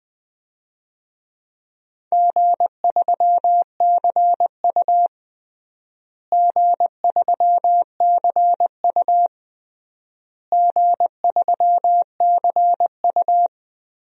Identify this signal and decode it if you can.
Morse code, heard as a keyed tone, about 20 wpm, 700 Hz. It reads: G3CU G3CU G3CU
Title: DXCC-100 Call Signs - 4 Characters - Repeated 3x - 20wpm